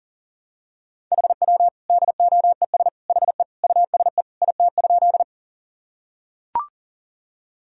Morse code, heard as a keyed tone, about 39 wpm, 700 Hz. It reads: HW DOES HE USE IT?
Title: Sentences from Top 200 Words and Abbreviations 40wpm